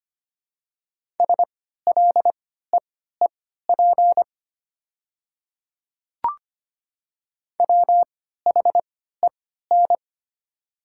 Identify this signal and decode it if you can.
Morse code, heard as a keyed tone, about 25 wpm, spaced as for 14 wpm, 700 Hz. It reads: SLEEP WHEN
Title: 3x Character Spacing - Top 500 Words - 25wpm